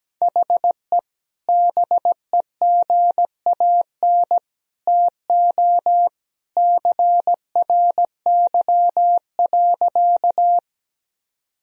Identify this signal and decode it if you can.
Morse code, heard as a keyed tone, about 17 wpm, 700 Hz. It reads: HE BEGAN TO CRY.
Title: Sentences from Top 1000 Words - Review Part 2 - 17wpm